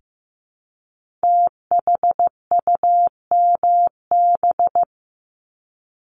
Morse code, heard as a keyed tone, about 15 wpm, 700 Hz. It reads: THUMB
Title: Multi-letter Phonemes Repeated 3x 15wpm